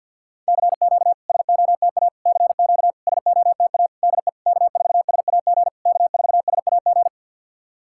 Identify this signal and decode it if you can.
Morse code, heard as a keyed tone, about 50 wpm, 700 Hz. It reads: CQ SOTA CQ SOTA DE K4SAG K4SAG
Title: SOTA QSO Practice 50wpm